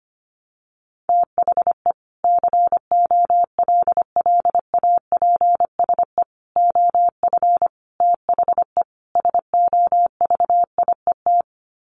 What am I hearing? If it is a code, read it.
Morse code, heard as a keyed tone, about 25 wpm, 700 Hz. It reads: THE COLLAPSE OF THE SOVIET
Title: Sets of 5 Words Encore 25wpm